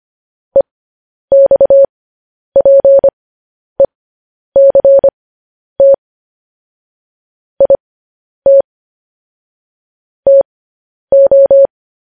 Morse code, heard as a keyed tone, about 25 wpm, spaced as for 10 wpm, 550 Hz. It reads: EXPECT IT TO
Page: News Headlines in Morse Code at 10 WPM